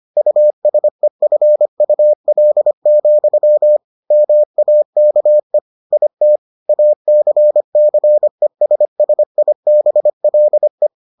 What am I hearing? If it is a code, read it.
Morse code, heard as a keyed tone, about 25 wpm, 600 Hz. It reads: USEFUL, MAKE IT ACCESSIBLE